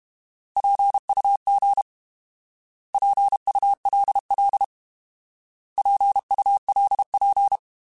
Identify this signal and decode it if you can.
Morse code, heard as a keyed tone, about 32 wpm, 800 Hz. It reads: PUG PULL PULP